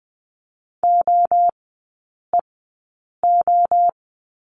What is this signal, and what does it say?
Morse code, heard as a keyed tone, about 20 wpm, spaced as for 8 wpm, 700 Hz. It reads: OEO
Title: 041 Lesson 8e - Number 1 - Three Characters 20wpm